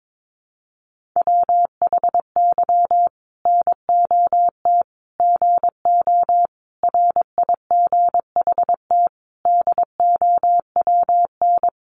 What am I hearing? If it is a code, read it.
Morse code, heard as a keyed tone, about 22 wpm, 700 Hz. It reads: WHY NOT GO RIGHT DOWN
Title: Sentences from Top 1400 Words 22wpm